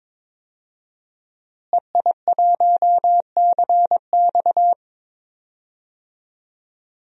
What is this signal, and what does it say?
Morse code, heard as a keyed tone, about 22 wpm, 700 Hz. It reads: EI1CX
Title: DXCC-100 Call Signs - 5 Characters - Repeated 2x - 22wpm